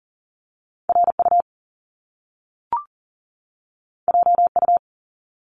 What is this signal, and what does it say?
Morse code, heard as a keyed tone, about 40 wpm, 700 Hz. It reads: RU JU
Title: Top Two Letter Combinations in Words 40wpm